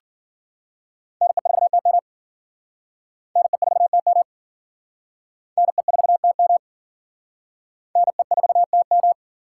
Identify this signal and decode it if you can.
Morse code, heard as a keyed tone, about 49 wpm, 700 Hz. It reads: NE4TM NE4TM NE4TM NE4TM
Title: Speed Racing - USA 2x2 Call Signs 40wpm